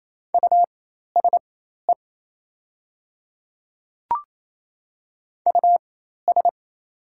Morse code, heard as a keyed tone, about 28 wpm, spaced as for 13 wpm, 700 Hz. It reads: USE US
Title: 4x Character Spacing - Top 300 Words - 28wpm